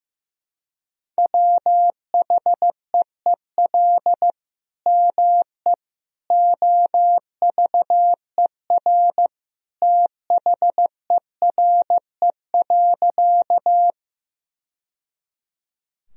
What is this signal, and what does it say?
Morse code, heard as a keyed tone, about 16 wpm, 700 Hz. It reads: WHEEL ME OVER THERE.
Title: Sentences from Top 1000 Words - Review Part 3 - 15wpm